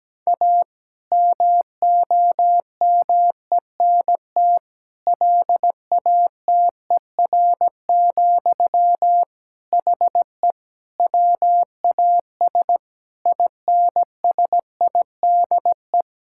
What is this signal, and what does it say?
Morse code, heard as a keyed tone, about 17 wpm, 700 Hz. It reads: A MOMENT LATER, HE WAS INSIDE
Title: Sentences from Top 1000 Words - Review Part 6 - 17wpm